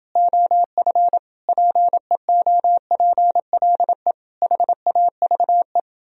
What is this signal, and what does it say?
Morse code, heard as a keyed tone, about 30 wpm, 700 Hz. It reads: OF PEOPLE HAVE